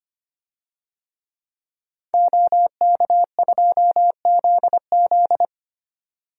Morse code, heard as a keyed tone, about 25 wpm, 700 Hz. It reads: OK2ZZ
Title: DXCC-100 Call Signs - 5 Characters - 25wpm